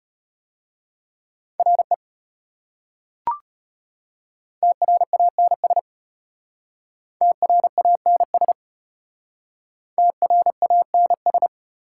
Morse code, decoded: RE TRANS TRANS TRANS

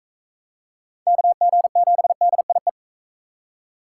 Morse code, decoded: KG7DIE

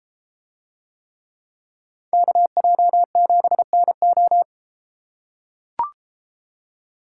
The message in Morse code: KJ7NO

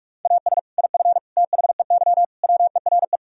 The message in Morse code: AS IF THEY WERE

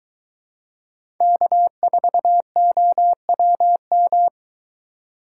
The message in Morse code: K4OWM

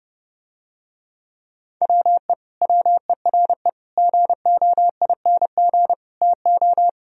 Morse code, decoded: WE WERE GOING TO

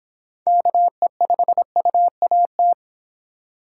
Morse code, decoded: KE5UAT

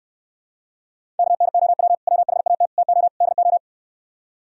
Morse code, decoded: KNOW WHAT TO DO

